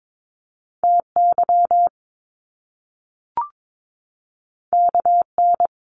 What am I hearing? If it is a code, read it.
Morse code, heard as a keyed tone, about 22 wpm, 700 Hz. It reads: TY KN
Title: Top Two Letter Combinations in Words 22wpm